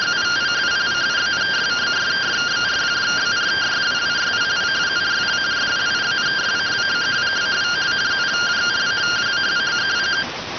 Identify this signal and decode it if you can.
Two-tone teleprinter signal, about 170 Hz shift.